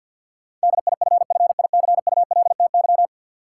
Morse code, heard as a keyed tone, about 53 wpm, 700 Hz. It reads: DIFFICULTY
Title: Sentences from Top 1500 Words - New Word Drill - 50wpm